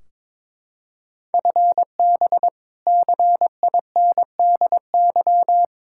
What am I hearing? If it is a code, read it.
Morse code, heard as a keyed tone, about 22 wpm, 700 Hz. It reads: FB CINDY
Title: Practice QSO - Exchange 3 - More Info - 22wpm